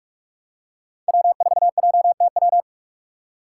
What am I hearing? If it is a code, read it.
Morse code, heard as a keyed tone, about 48 wpm, 700 Hz. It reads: W4JTW
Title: USA Call Signs - 5 Characters - 6x Character Spacing - 45wpm